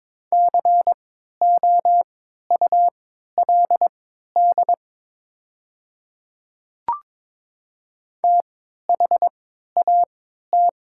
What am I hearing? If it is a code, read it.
Morse code, heard as a keyed tone, about 22 wpm, spaced as for 13 wpm, 700 Hz. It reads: COULD THAT
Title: 3x Character Spacing - Top 100 Words - 22wpm